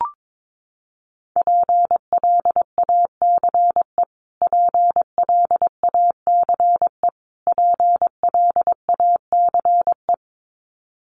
Morse code, heard as a keyed tone, about 22 wpm, 700 Hz. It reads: PLACE PLACE PLACE
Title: Top 200 Words Repeated 22wpm